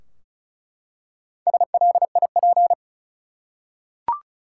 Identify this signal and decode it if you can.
Morse code, heard as a keyed tone, about 34 wpm, 700 Hz. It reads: SLIP
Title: Four Letter Words - 35wpm